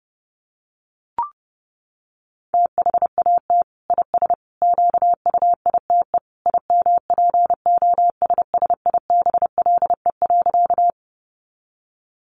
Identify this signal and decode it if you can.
Morse code, heard as a keyed tone, about 30 wpm, 700 Hz. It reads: THAT IS QUITE IMPOSSIBLE.
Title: Sentences from Top 1100 Words 30wpm